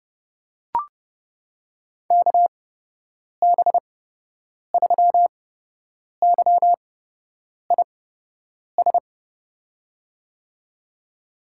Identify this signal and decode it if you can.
Morse code, heard as a keyed tone, about 30 wpm, spaced as for 8 wpm, 700 Hz. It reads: KB3YIS
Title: USA Call Signs - 8x Character Spacing - 30wpm